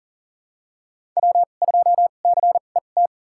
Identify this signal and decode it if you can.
Morse code, heard as a keyed tone, about 37 wpm, spaced as for 28 wpm, 700 Hz. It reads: W2CET
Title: USA Call Signs - 5 Characters - 2x Character Spacing - 40wpm